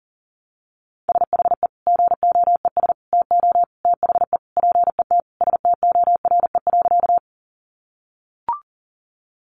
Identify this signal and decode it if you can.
Morse code, heard as a keyed tone, about 40 wpm, 700 Hz. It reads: SHE GOES TO THE PET STORE.
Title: Easy Phrases 40wpm